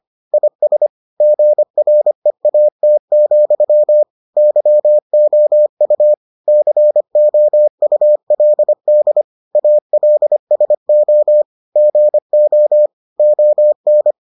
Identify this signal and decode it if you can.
Morse code, heard as a keyed tone, about 25 wpm, 600 Hz. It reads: IS GREAT, YOU COULD ALSO GO ON